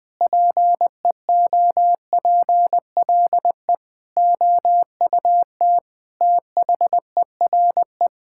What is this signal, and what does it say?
Morse code, heard as a keyed tone, about 21 wpm, 700 Hz. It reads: PEOPLE OUT THERE
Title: Sets of 8 Words - 20wpm